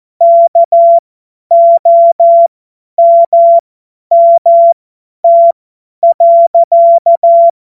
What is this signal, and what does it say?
Morse code, heard as a keyed tone, about 14 wpm, spaced as for 10 wpm, 680 Hz. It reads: KOMMT.
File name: Morsekurs / Witze / Witz 21. Juni 2022 50 F70